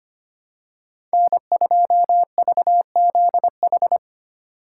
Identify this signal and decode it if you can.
Morse code, heard as a keyed tone, about 25 wpm, 700 Hz. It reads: N2VZH
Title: USA Call Signs - 5 Characters - 2x Character Spacing - 25wpm